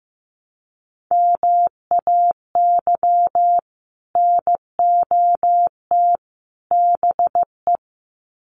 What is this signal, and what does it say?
Morse code, heard as a keyed tone, about 15 wpm, 700 Hz. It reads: MAY NOT BE
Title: Sets of 3 Words 15wpm